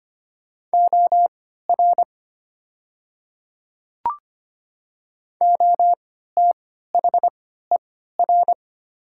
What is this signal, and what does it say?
Morse code, heard as a keyed tone, about 25 wpm, spaced as for 14 wpm, 700 Hz. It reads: OR OTHER
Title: 3x Character Spacing - Top 100 Words - 25wpm